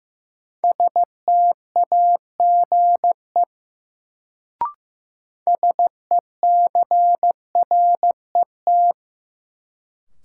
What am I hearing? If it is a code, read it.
Morse code, heard as a keyed tone, about 15 wpm, 700 Hz. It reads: STAGE SECRET
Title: Sentences from Top 1200 Words - New Word Drill - 15wpm